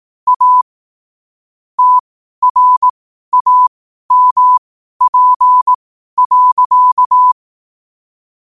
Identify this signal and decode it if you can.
Morse code, heard as a keyed tone, about 18 wpm, spaced as for 12 wpm, 1000 Hz. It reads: A TRAMP.
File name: qotdfiles / 12wpm